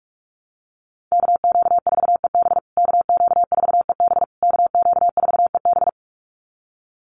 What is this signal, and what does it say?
Morse code, heard as a keyed tone, about 44 wpm, 690 Hz. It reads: KQ4EB KQ4EB KQ4EB
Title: USA Call Signs Repeated 3x 45wpm